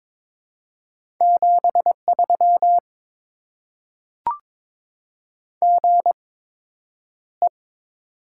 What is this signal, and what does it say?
Morse code, heard as a keyed tone, about 22 wpm, spaced as for 9 wpm, 700 Hz. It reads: 73 GE
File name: Top 20 QSO Elements - 8x Character Spacing - 22wpm